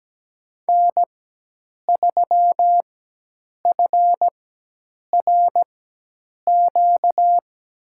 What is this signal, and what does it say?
Morse code, heard as a keyed tone, about 17 wpm, spaced as for 8 wpm, 700 Hz. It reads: N3FRQ
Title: USA Call Signs - 5 Characters - 4x Character Spacing - 17wpm